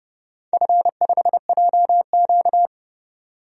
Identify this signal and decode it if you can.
Morse code, heard as a keyed tone, about 30 wpm, 700 Hz. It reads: F5JQ
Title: DXCC-100 Call Signs - 4 Characters - 30wpm